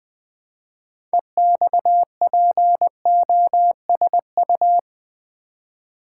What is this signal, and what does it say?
Morse code, heard as a keyed tone, about 20 wpm, 700 Hz. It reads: EXPOSU
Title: Build A Word - 2 Letters At A Time - 8-Letter Words - 20wpm